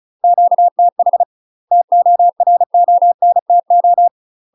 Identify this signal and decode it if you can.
Morse code, heard as a keyed tone, about 36 wpm, 700 Hz. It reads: QTH TORONTO
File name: UK MASTER QSOs TRAINING / 35WPM / UK MASTER QSOs 0002